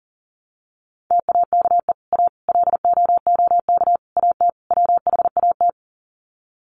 Morse code, heard as a keyed tone, about 40 wpm, 700 Hz. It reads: TAKE A LOOK AT WHAT
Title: Sets of 5 Words Encore2 40wpm